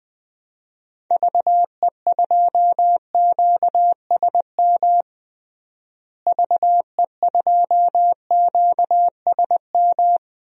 Morse code, heard as a keyed tone, about 20 wpm, 700 Hz. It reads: VE2QSM VE2QSM
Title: DXCC-100 Call Signs - 6 Characters - Repeated 2x - 20wpm